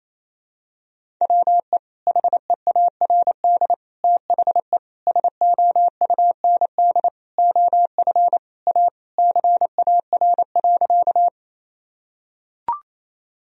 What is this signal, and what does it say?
Morse code, heard as a keyed tone, about 28 wpm, 700 Hz. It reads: WE HEARD THE SOUND OF A CAR.
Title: Sentences from Top 1000 Words - Review Part 1 - 28wpm